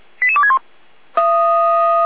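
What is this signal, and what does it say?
Five-tone selective call (ZVEI), heard as quick runs of five tones, tones of about 70 ms. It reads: 88261
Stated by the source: ZVEI five-tone selective call 88333 received on SDR.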